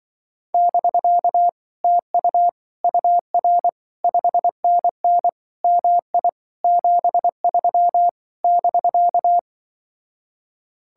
Morse code, heard as a keyed tone, about 25 wpm, 700 Hz. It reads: <BK> TU UR 5NN MI 73 <BK>
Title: Speed-Racing Common QSO Phrases - 20wpm